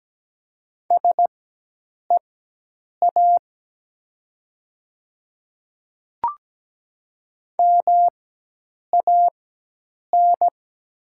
Morse code, heard as a keyed tone, about 17 wpm, spaced as for 8 wpm, 700 Hz. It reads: SEA MAN
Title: 4x Character Spacing - Top 500 Words - 17wpm